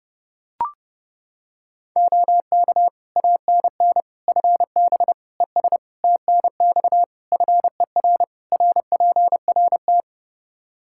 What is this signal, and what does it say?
Morse code, heard as a keyed tone, about 30 wpm, 700 Hz. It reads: OK ANN FB ES TNX FER RPRT
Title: K1USN SST QSO Practice 30wpm